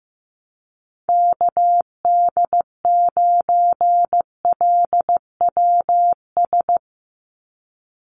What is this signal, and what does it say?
Morse code, heard as a keyed tone, about 15 wpm, 700 Hz. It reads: KD9LWS
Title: USA Call Signs 15wpm